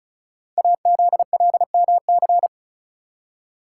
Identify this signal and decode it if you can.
Morse code, heard as a keyed tone, about 36 wpm, 700 Hz. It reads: AZLMC
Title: Code Groups - 5 Letters - 35wpm